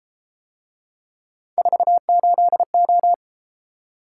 Morse code, heard as a keyed tone, about 31 wpm, 700 Hz. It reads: 48O